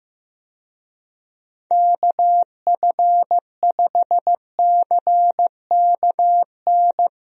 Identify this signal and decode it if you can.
Morse code, heard as a keyed tone, about 15 wpm, 700 Hz. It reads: KF5CKN
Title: USA Call Signs - 6 Characters - Repeated 2x - 15wpm